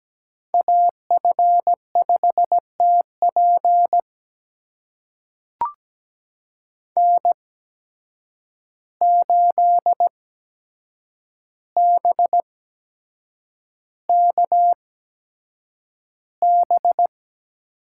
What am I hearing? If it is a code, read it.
Morse code, heard as a keyed tone, about 17 wpm, spaced as for 7 wpm, 700 Hz. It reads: AF5TP N8BKB
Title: USA Call Signs - 5 Characters - 8x Character Spacing - 17wpm